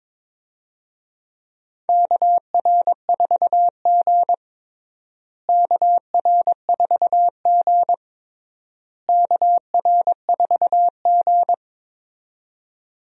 Morse code, heard as a keyed tone, about 21 wpm, 700 Hz. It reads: KR4G KR4G KR4G
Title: USA Call Signs - 4 Characters - Repeated 3x - 22wpm